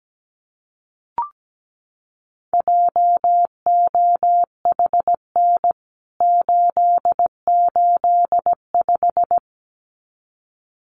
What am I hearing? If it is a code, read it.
Morse code, heard as a keyed tone, about 17 wpm, 700 Hz. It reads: JOHN 885
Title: CWT Exchange 17wpm